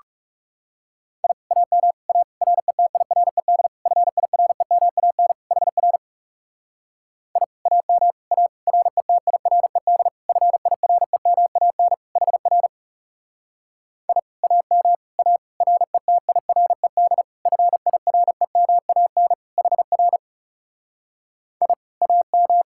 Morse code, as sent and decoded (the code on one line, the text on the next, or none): .. / .- -- / .- / .-. - .. .-. -.. / ..-. .. .-. -- .- -. / .... .-. / .. / .- -- / .- / .-. - .. .-. -.. / ..-. .. .-. -- .- -. / .... .-. / .. / .- -- / .- / .-. - .. .-. -.. / ..-. .. .-. -- .- -. / .... .-. / .. / .- --
I AM A RETIRED FIREMAN HR I AM A RETIRED FIREMAN HR I AM A RETIRED FIREMAN HR I AM